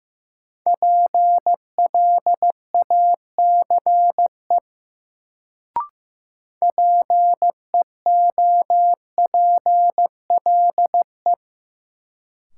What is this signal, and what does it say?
Morse code, heard as a keyed tone, about 15 wpm, 700 Hz. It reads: PLACE PEOPLE
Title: Sentences from Top 200 Words - New Word Drill - 15wpm